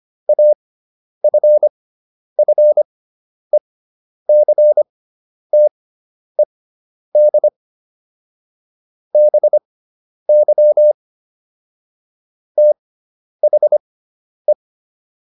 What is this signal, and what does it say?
Morse code, heard as a keyed tone, about 25 wpm, spaced as for 10 wpm, 600 Hz. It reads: AFFECTED BY THE